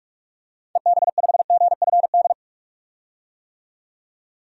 Morse code, heard as a keyed tone, about 45 wpm, 700 Hz. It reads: EB5GFD
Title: DXCC-100 Call Signs - 6 Characters - 45wpm